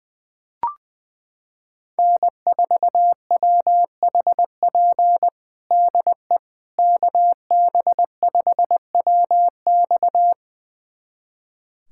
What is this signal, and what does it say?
Morse code, heard as a keyed tone, about 20 wpm, 700 Hz. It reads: N4WHP DE KB5WX
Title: USA Callsign DE Callsign - 5 Characters - 20wpm